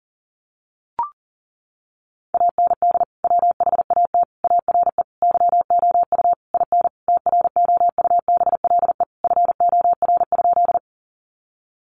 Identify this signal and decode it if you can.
Morse code, heard as a keyed tone, about 40 wpm, 700 Hz. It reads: AND WHAT ARE YOU IN TROUBLE FOR?
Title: Sentences from Top 1200 Words 40wpm